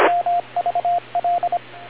Morse code, heard as a keyed tone, about 26 wpm, 700 Hz, with radio noise. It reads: MVL